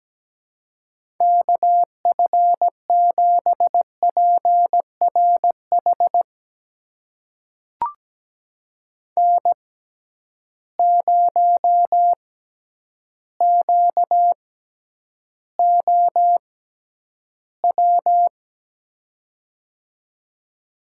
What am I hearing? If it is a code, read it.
Morse code, heard as a keyed tone, about 17 wpm, spaced as for 9 wpm, 700 Hz. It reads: KF7PRH N0QOW
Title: USA Call Signs - 6x Character Spacing - 17wpm